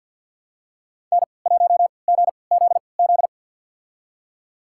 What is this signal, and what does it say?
Morse code, heard as a keyed tone, about 47 wpm, spaced as for 28 wpm, 700 Hz. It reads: N1GZZ
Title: USA Call Signs - 5 Characters - 3x Character Spacing - 50wpm